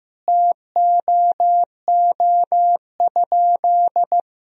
Morse code, heard as a keyed tone, about 15 wpm, 700 Hz. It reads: TOO?
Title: Sentences from Top 200 Words and Abbreviations 15wpm